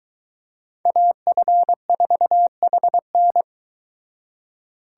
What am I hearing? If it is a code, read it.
Morse code, heard as a keyed tone, about 23 wpm, 700 Hz. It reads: AF4HN